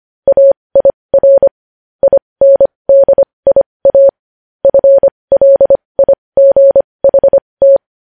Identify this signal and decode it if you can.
Morse code, heard as a keyed tone, about 25 wpm, spaced as for 20 wpm, 550 Hz. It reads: AIR INDIA FLIGHT
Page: news-headlines-in-morse-code-at-20-wpm